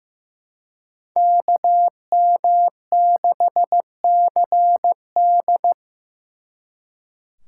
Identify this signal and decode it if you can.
Morse code, heard as a keyed tone, about 18 wpm, 700 Hz. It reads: KM6CD
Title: Speed Racing - USA 2x2 Call Signs 15wpm